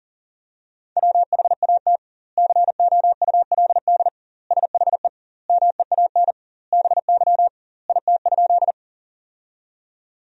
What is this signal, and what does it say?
Morse code, heard as a keyed tone, about 40 wpm, 700 Hz. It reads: WHAT COULD SHE MEAN BY IT?
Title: 2x Word-Spacing - Sentences from Top 200 Words - 40wpm